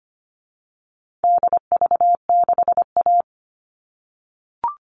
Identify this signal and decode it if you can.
Morse code, heard as a keyed tone, about 25 wpm, 700 Hz. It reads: DV6A
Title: All Cut Numbers - 4-Digit 25wpm